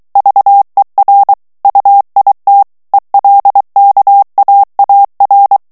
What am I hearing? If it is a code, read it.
Morse code, heard as a keyed tone, about 23 wpm, 800 Hz. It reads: VER UIT ELKAAR
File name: HOLLANDAIS texte clair 1 vitesse 22 mots minute